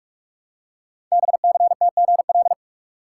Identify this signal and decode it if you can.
Morse code, heard as a keyed tone, about 44 wpm, 700 Hz. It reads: BCTGL
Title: Code Groups - 5 Letters - 45wpm